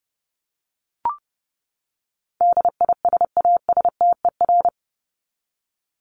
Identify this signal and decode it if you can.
Morse code, heard as a keyed tone, about 30 wpm, 700 Hz. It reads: DISASTER
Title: Top 1501-1750 Words 30wpm